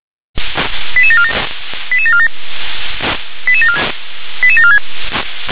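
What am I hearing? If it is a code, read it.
Five-tone selective call (ZVEI), heard as quick runs of five tones, tones of about 70 ms. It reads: 88747 88747 88746
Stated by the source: ZVEI five-tone call received by SDR.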